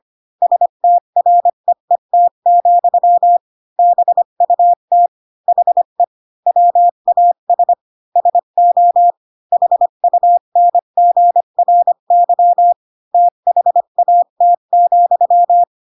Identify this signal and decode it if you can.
Morse code, heard as a keyed tone, about 25 wpm, 700 Hz. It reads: STREET, BUT HE WAS SO HUNGRY THAT,
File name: Pinocchio / Pinocchio ch6 20 25 ik1hge